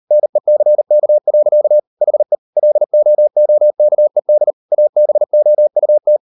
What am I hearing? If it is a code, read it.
Morse code, heard as a keyed tone, about 38 wpm, 600 Hz. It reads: NECK. HE LOOKED ABOUT